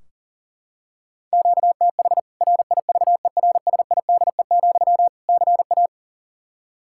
Morse code, heard as a keyed tone, about 40 wpm, 700 Hz. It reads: QTH RIVERSIDE, CA
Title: K1USN SST QSO Practice 40wpm